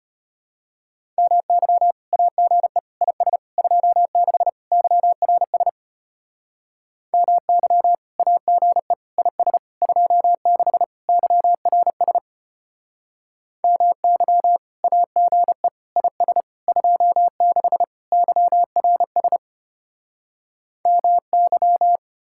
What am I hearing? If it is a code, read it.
Morse code, heard as a keyed tone, about 33 wpm, 700 Hz. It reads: MY AGE IS 26 YRS MY AGE IS 26 YRS MY AGE IS 26 YRS MY